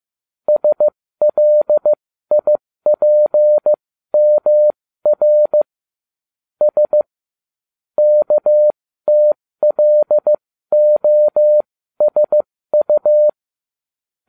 Morse code, heard as a keyed tone, about 15 wpm, spaced as for 13 wpm, 600 Hz. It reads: SLIPMR S KTLOSU